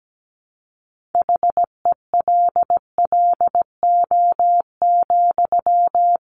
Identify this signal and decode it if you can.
Morse code, heard as a keyed tone, about 17 wpm, 700 Hz. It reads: HELLO,
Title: Sentences from Top 1100 Words 17wpm